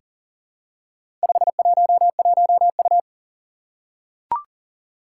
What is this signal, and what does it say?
Morse code, heard as a keyed tone, about 38 wpm, 700 Hz. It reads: 511U
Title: Four Challenging Letter-Number Combinations 40wpm